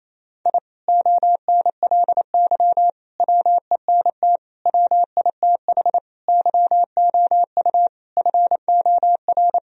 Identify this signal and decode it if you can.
Morse code, heard as a keyed tone, about 28 wpm, 700 Hz. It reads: I ONLY WENT WITH YOU FOR
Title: Sentences from Top 1000 Words - Review Part 1 - 28wpm